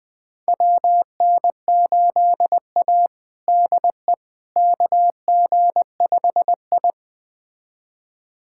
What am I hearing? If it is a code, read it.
Morse code, heard as a keyed tone, about 20 wpm, 700 Hz. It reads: WN8A DE KG5I
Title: USA Callsign DE Callsign - 4 Characters - 20wpm